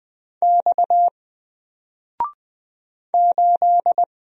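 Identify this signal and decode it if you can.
Morse code, heard as a keyed tone, about 20 wpm, 700 Hz. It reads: X 8
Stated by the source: Single letter and number Morse drill at 20 wpm, from N4LCD.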